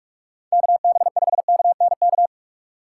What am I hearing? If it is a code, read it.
Morse code, heard as a keyed tone, about 44 wpm, 700 Hz. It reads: KB5KNK
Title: USA Call Signs - 6 Characters - 4x Character Spacing - 45wpm